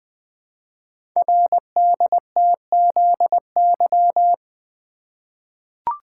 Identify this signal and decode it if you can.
Morse code, heard as a keyed tone, about 20 wpm, 700 Hz. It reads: RDTZY
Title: Code Groups - 5 Letters - 20wpm